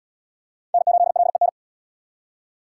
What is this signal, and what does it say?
Morse code, heard as a keyed tone, about 69 wpm, 700 Hz. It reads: N8LEN